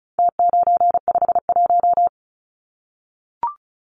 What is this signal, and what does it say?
Morse code, heard as a keyed tone, about 34 wpm, 700 Hz. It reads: T951
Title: Numbers - Contest Style - 4-Digit 35wpm